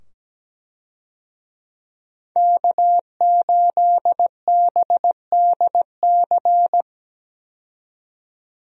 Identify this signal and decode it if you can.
Morse code, heard as a keyed tone, about 17 wpm, 700 Hz. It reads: K8BDC
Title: USA Call Signs - 5 Characters - 17wpm